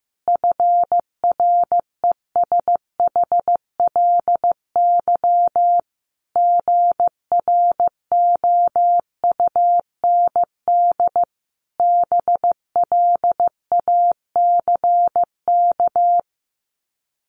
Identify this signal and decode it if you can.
Morse code, heard as a keyed tone, about 15 wpm, 700 Hz. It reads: FRESHLY GROUND BLACK
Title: Sets of 5 Words Encore 15wpm